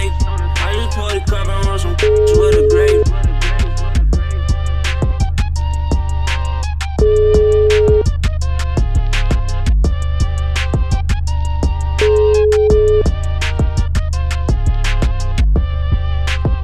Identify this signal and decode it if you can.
Ringback tone (Europe, 425 Hz).